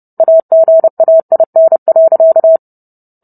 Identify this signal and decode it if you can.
Morse code, heard as a keyed tone, about 30 wpm, 650 Hz. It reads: AGAIN.